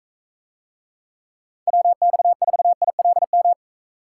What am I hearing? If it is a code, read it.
Morse code, heard as a keyed tone, about 41 wpm, 700 Hz. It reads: WX4ILM